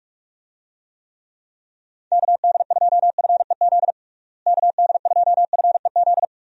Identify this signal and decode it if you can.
Morse code, heard as a keyed tone, about 46 wpm, 700 Hz. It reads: KD2FEZ KD2FEZ
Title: USA Call Signs - 6 Characters - Repeated 2x - 45wpm